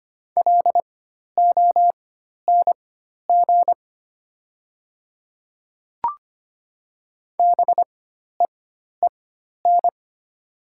Morse code, heard as a keyed tone, about 25 wpm, spaced as for 12 wpm, 700 Hz. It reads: LONG BEEN
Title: 4x Character Spacing - Top 300 Words - 25wpm